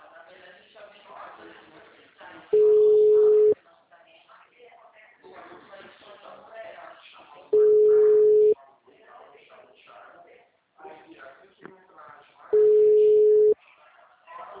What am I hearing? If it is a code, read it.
Ringback tone (Europe, 425 Hz), with speech.